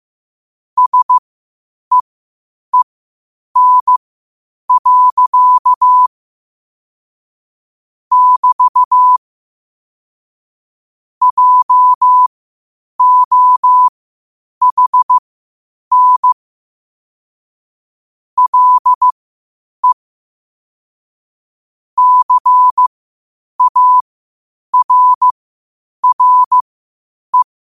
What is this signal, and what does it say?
Morse code, heard as a keyed tone, about 15 wpm, spaced as for 8 wpm, 1000 Hz. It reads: SEEN. <BT> JOHN LE CARRE
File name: 8wpm